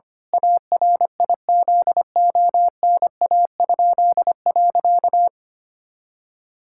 Morse code, heard as a keyed tone, about 25 wpm, 700 Hz. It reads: ARIZONA?.
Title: Pangram Challenge v2 - 25wpm